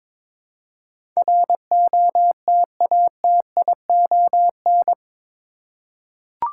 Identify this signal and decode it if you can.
Morse code, heard as a keyed tone, about 22 wpm, 700 Hz. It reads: ROTATION
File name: practice / 22wpm / Eight Letter Words - 22wpm